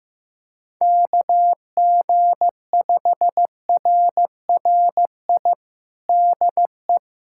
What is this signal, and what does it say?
Morse code, heard as a keyed tone, about 15 wpm, 700 Hz. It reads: KG5RRI DE
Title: USA Callsign DE Callsign - 6 Characters - 15wpm